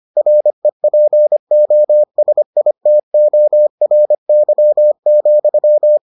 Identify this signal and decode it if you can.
Morse code, heard as a keyed tone, about 25 wpm, 600 Hz. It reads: REPOSITORY,